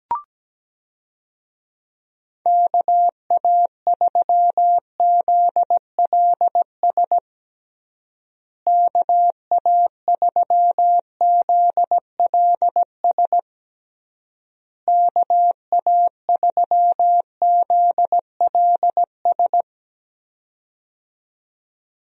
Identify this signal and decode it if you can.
Morse code, heard as a keyed tone, about 17 wpm, 700 Hz. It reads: KA3ZLS KA3ZLS KA3ZLS
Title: USA Call Signs - 6 Characters - Repeated 3x - 17wpm